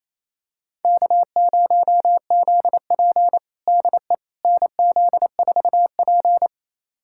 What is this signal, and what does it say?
Morse code, heard as a keyed tone, about 28 wpm, 700 Hz. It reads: K0ZP DE NZ4P
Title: USA Callsign DE Callsign - 4 Characters - 28wpm